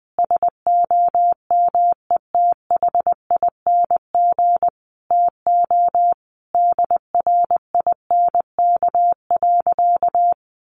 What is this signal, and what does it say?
Morse code, heard as a keyed tone, about 20 wpm, 700 Hz. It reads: SOMETHING TO DRINK.
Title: Sentences from Top 1500 Words 20wpm